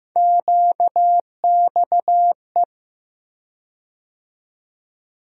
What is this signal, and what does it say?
Morse code, heard as a keyed tone, about 15 wpm, 700 Hz. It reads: QXE
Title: USA Call Sign Postfixes - 3 Letters 15wpm